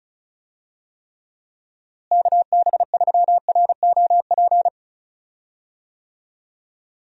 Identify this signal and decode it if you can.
Morse code, heard as a keyed tone, about 34 wpm, 700 Hz. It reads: KB3ROP
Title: USA Call Signs - 6 Characters - Repeated 3x - 35wpm